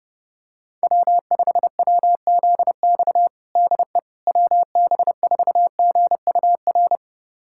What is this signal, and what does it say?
Morse code, heard as a keyed tone, about 30 wpm, 700 Hz. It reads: W5WZX DE WB4GUR
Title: USA Callsign DE Callsign 30wpm